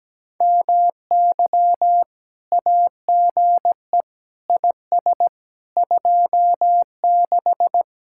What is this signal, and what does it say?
Morse code, heard as a keyed tone, about 17 wpm, 700 Hz. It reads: MY AGE IS 26
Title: Speed-Racing Common QSO Phrases Encore - 17wpm